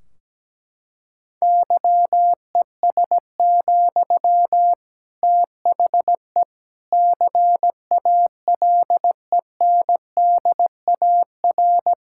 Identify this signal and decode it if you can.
Morse code, heard as a keyed tone, about 17 wpm, 700 Hz. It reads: YES, THE CALENDAR
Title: Sentences from Top 1600 Words 17wpm